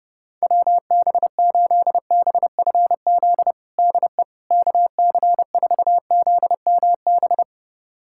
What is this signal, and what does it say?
Morse code, heard as a keyed tone, about 30 wpm, 700 Hz. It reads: WB8BFZ DE KC4ZMB
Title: USA Callsign DE Callsign - 6 Characters - 30wpm